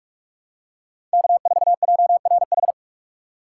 Morse code, heard as a keyed tone, about 45 wpm, 700 Hz. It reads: K4JRH
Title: USA Call Signs - 5 Characters - 6x Character Spacing - 45wpm